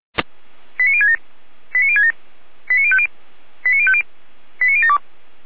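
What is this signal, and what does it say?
Five-tone selective call (ZVEI), heard as quick runs of five tones, tones of about 80 ms. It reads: 89067 89067 89050 89050 89072
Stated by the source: ZVEI five-tone call 89012 received on SDR.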